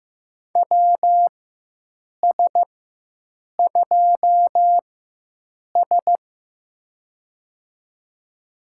Morse code, heard as a keyed tone, about 15 wpm, spaced as for 7 wpm, 700 Hz. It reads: WS2S
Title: USA Call Signs - 4x Character Spacing - 15wpm